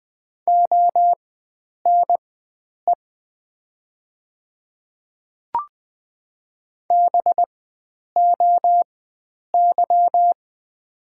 Morse code, heard as a keyed tone, about 20 wpm, spaced as for 9 wpm, 700 Hz. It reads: ONE BOY
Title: 4x Character Spacing - Top 300 Words - 20wpm